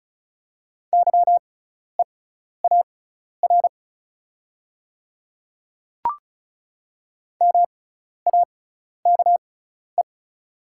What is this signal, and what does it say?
Morse code, heard as a keyed tone, about 34 wpm, spaced as for 12 wpm, 700 Hz. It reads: YEAR MAKE
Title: 6x Character Spacing - Top 200 Words - 35wpm